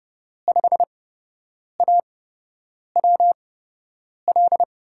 Morse code, heard as a keyed tone, about 30 wpm, spaced as for 8 wpm, 700 Hz. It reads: 5AWL
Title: USA Call Signs - 6 Characters - 8x Character Spacing - 30wpm